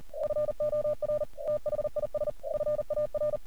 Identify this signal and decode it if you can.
Morse code, heard as a keyed tone, about 40 wpm, 600 Hz. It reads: FOR THIS FAR
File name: archived / CRSnet